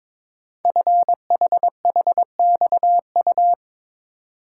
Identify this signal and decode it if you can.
Morse code, heard as a keyed tone, about 22 wpm, 700 Hz. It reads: FHHXU
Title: Code Groups - 5 Letters - 22wpm